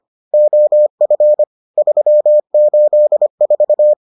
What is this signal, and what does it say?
Morse code, heard as a keyed tone, about 25 wpm, 600 Hz. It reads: OF 384